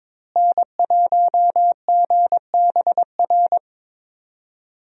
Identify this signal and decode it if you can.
Morse code, heard as a keyed tone, about 22 wpm, 700 Hz. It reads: N1GBR
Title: USA Call Signs - 5 Characters - Repeated 3x - 22wpm